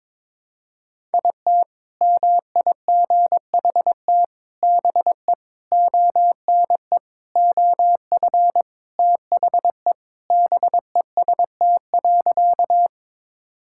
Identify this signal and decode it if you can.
Morse code, heard as a keyed tone, about 22 wpm, 700 Hz. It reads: IT MIGHT BE ONE OF THE BEST.
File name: practice / 22wpm / Sentences with 4-Word Phrases - 22wpm